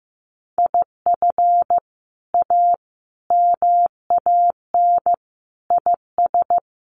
Morse code, heard as a keyed tone, about 14 wpm, 700 Hz. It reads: IF A MAN IS
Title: Sentences from Top 1400 Words 15wpm